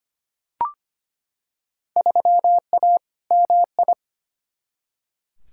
Morse code, heard as a keyed tone, about 25 wpm, 700 Hz. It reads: 3A MI
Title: Field Day Exchanges - 25wpm